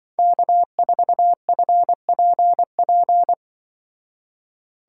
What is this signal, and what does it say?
Morse code, heard as a keyed tone, about 24 wpm, 700 Hz. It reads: K4FPP
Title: Speed Racing - USA Call Signs 20wpm